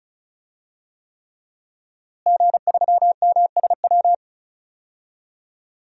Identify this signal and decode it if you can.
Morse code, heard as a keyed tone, about 36 wpm, 700 Hz. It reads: G3MSW
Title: DXCC-100 Call Signs - 5 Characters - Repeated 3x - 35wpm